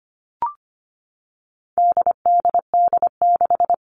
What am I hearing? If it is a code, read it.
Morse code, heard as a keyed tone, about 25 wpm, 700 Hz. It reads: DDD6
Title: All Cut Numbers - 4-Digit 25wpm